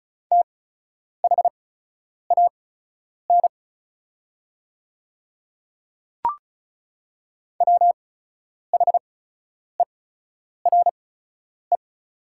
Morse code, heard as a keyed tone, about 36 wpm, spaced as for 10 wpm, 700 Hz. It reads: THAN WHERE
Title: 8x Character Spacing - Top 200 Words - 35wpm